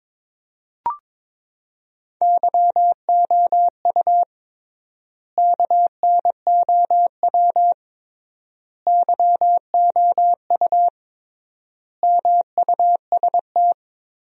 Morse code, heard as a keyed tone, about 22 wpm, 700 Hz. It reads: YOU KNOW YOU MUST
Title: 3x Word-Spacing - Sentences from Top 500 Words - 22wpm